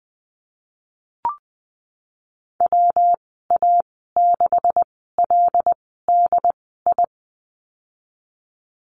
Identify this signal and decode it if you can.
Morse code, heard as a keyed tone, about 20 wpm, spaced as for 15 wpm, 700 Hz. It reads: WA6LDI
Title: 167 Lesson 27g - Number 7 - Call Signs 20wpm